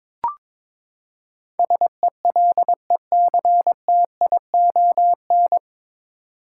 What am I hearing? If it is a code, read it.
Morse code, heard as a keyed tone, about 22 wpm, 700 Hz. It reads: SELECTION
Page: Morse Code Ninja practice 22wpm